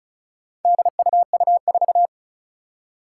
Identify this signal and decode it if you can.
Morse code, heard as a keyed tone, about 36 wpm, 700 Hz. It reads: DUU4
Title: Four Challenging Letter-Number Combinations 35wpm